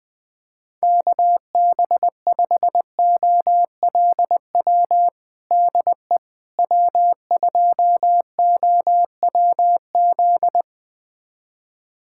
Morse code, heard as a keyed tone, about 20 wpm, 700 Hz. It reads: KB5OLW DE W2OWZ
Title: USA Callsign DE Callsign 20wpm